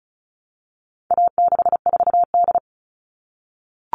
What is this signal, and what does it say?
Morse code, heard as a keyed tone, about 34 wpm, 700 Hz. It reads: A64D
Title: All Cut Numbers - 4-Digit 35wpm